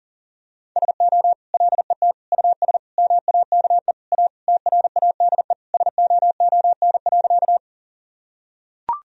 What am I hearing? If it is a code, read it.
Morse code, heard as a keyed tone, about 40 wpm, 700 Hz. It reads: SO LET US MAKE A TRADE SOON.
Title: Sentences from Top 1000 Words - Review Part 4 - 40wpm